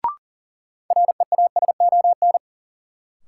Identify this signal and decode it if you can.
Morse code, heard as a keyed tone, about 40 wpm, 700 Hz. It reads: REASON
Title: Sets of 5 Words Encore - New Word Drill - 40wpm